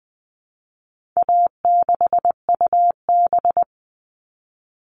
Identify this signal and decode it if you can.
Morse code, heard as a keyed tone, about 19 wpm, 700 Hz. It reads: A6UB
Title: All Cut Numbers - 4-Digit 20wpm